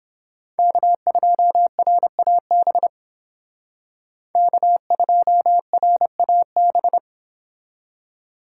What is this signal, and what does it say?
Morse code, heard as a keyed tone, about 31 wpm, 700 Hz. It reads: K2RAB K2RAB